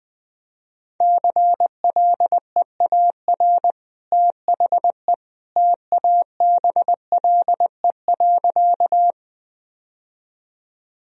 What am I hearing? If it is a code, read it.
Morse code, heard as a keyed tone, about 20 wpm, 700 Hz. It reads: CLEAR THE TABLE.
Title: Sentences from Top 1000 Words - Review Part 2 - 20wpm